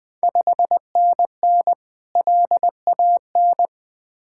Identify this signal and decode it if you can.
Morse code, heard as a keyed tone, about 21 wpm, 700 Hz. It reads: 5NN LAN
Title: Ontario QSO Party Exchange 20wpm